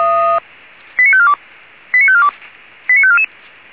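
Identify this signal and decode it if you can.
Five-tone selective call (ZVEI), heard as quick runs of five tones, tones of about 70 ms. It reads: 89542 89542 89544
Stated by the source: ZVEI five-tone call 89131 received by SDR.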